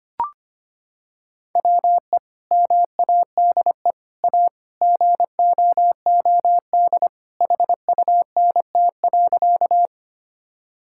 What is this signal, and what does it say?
Morse code, heard as a keyed tone, about 25 wpm, 700 Hz. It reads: WE MADE A GOOD HUNT.
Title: Sentences from Top 1600 Words 25wpm